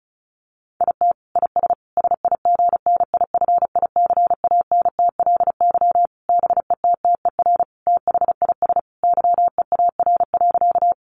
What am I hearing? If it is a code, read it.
Morse code, heard as a keyed tone, about 34 wpm, 700 Hz. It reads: IT IS SIGNIFICANTLY BETTER THIS YEAR.
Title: Sentences from Top 1500 Words 35wpm